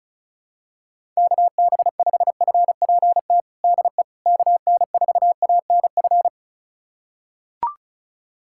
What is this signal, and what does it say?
Morse code, heard as a keyed tone, about 36 wpm, 700 Hz. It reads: KB5FPT DE KN4ANF
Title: USA Callsign DE Callsign - 6 Characters - 35wpm